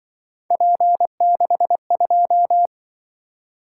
Morse code, heard as a keyed tone, about 23 wpm, 700 Hz. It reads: P62
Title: Speed-Racing - Three Letter-Number Combinations - 20wpm